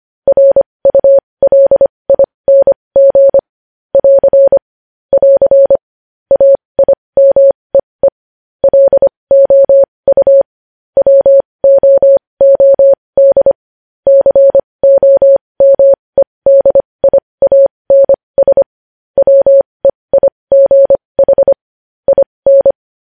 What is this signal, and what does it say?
Morse code, heard as a keyed tone, about 25 wpm, spaced as for 20 wpm, 550 Hz. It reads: RULING <AR> <AR> AIMEE LOU WOOD COMEDIANS WEIGH IN
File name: podcast / News headlines-1x-25x20